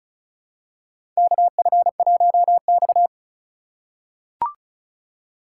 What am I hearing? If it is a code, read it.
Morse code, heard as a keyed tone, about 36 wpm, 700 Hz. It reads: KF1X
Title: USA Call Signs - 4 Characters - 2x Character Spacing - 35wpm